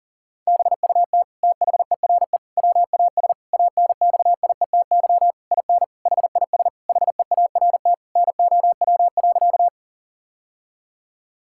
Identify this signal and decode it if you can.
Morse code, heard as a keyed tone, about 40 wpm, 700 Hz. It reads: BUT THERE WAS ANXIETY IN HIS HEART NOW.
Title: Sentences from Top 1600 Words 40wpm